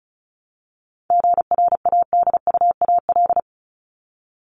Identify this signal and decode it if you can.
Morse code, heard as a keyed tone, about 34 wpm, 700 Hz. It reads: GRADUAL
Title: LY Suffix - Top 100 Words with Base Words 35wpm